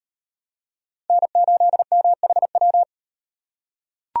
Morse code, heard as a keyed tone, about 38 wpm, 700 Hz. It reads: N8MHW